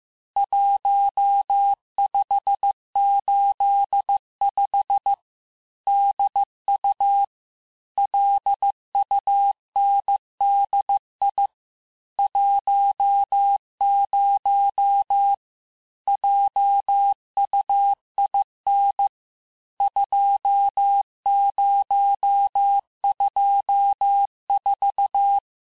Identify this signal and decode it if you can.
Morse code, heard as a keyed tone, about 15 wpm, 800 Hz. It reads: 1585 DU LUNDI 10 JUIN 2024
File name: cours-du-lundi files / cours-1585-texte-chiffre-vitesse-14mpm